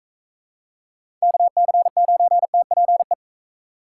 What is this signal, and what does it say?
Morse code, heard as a keyed tone, about 42 wpm, 700 Hz. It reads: KC9TPE